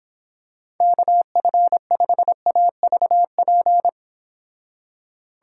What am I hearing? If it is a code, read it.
Morse code, heard as a keyed tone, about 26 wpm, 700 Hz. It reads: KF5AVP